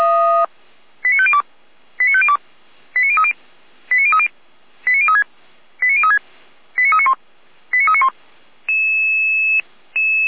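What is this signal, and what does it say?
Five-tone selective call (ZVEI), heard as quick runs of five tones, tones of about 70 ms. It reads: 89502 89502 89030 89030 89036 89036 89391 89391
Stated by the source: Received on an SDR, ZVEI five-tone code 89050.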